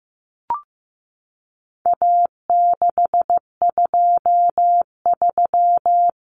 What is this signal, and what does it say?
Morse code, heard as a keyed tone, about 15 wpm, 700 Hz. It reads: A623
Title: Common Cut Numbers - 4-Digit 15wpm